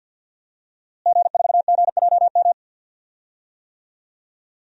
Morse code, heard as a keyed tone, about 52 wpm, 700 Hz. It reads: G4GJM
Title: DXCC-100 Call Signs - 5 Characters - Repeated 2x - 50wpm